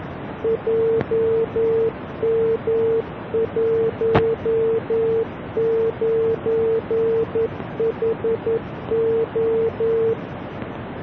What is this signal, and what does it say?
Morse code, heard as a keyed tone, about 11 wpm, 450 Hz, with noise. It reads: JM19HO